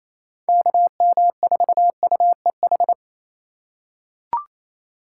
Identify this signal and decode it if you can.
Morse code, heard as a keyed tone, about 28 wpm, 700 Hz. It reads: KM4UEH
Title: USA Call Signs - 6 Characters - 4x Character Spacing - 28wpm